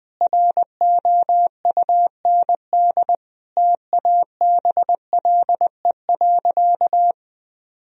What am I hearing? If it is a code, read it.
Morse code, heard as a keyed tone, about 20 wpm, 700 Hz. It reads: ROUND TABLE.
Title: Sentences from Top 1000 Words - Review Part 2 - 20wpm